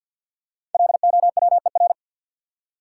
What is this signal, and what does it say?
Morse code, heard as a keyed tone, about 53 wpm, 700 Hz. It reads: LOWER
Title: Five Letter Words - 50wpm